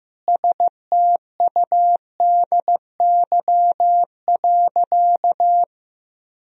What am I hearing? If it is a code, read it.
Morse code, heard as a keyed tone, about 15 wpm, 700 Hz. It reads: STUDY.
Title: Sentences from Top 1000 Words - Review Part 1 - 15wpm